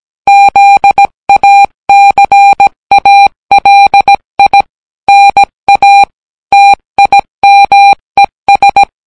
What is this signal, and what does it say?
Morse code, heard as a keyed tone, about 17 wpm, 800 Hz. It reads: ZACALI NA TIMES